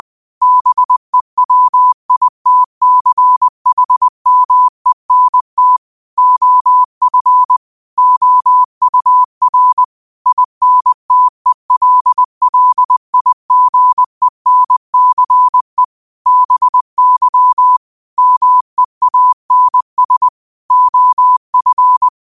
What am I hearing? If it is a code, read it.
Morse code, heard as a keyed tone, about 20 wpm, 1000 Hz. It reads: BEWITCHMENT OF OUR INTELLIGENCE BY MEANS OF